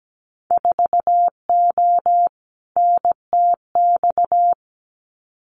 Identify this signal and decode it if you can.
Morse code, heard as a keyed tone, about 17 wpm, 700 Hz. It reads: 4O NTX
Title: Winter Field Day Exchanges 17wpm